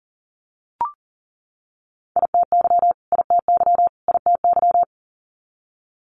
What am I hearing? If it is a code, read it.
Morse code, heard as a keyed tone, about 40 wpm, 700 Hz. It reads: ITY ITY ITY
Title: Multi-letter Phonemes Repeated 3x 40wpm